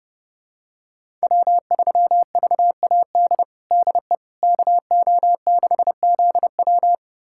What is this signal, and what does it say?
Morse code, heard as a keyed tone, about 30 wpm, 700 Hz. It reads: W3VAD DE KO6ZW
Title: USA Callsign DE Callsign - 5 Characters - 30wpm